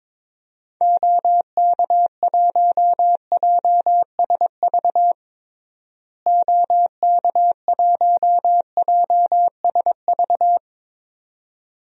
Morse code, heard as a keyed tone, about 22 wpm, 700 Hz. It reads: OK1JSV OK1JSV
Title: DXCC-100 Call Signs - 6 Characters - Repeated 2x - 22wpm